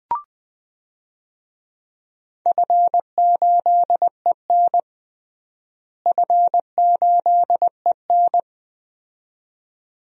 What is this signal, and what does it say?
Morse code, heard as a keyed tone, about 20 wpm, 700 Hz. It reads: F8EN F8EN
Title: DXCC-100 Call Signs - 4 Characters - Repeated 2x - 20wpm